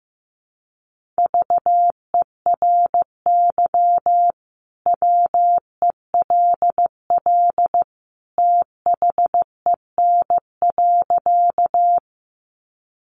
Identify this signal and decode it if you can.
Morse code, heard as a keyed tone, about 15 wpm, 700 Hz. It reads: VERY WELL THEN.
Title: Sentences from Top 200 Words 15wpm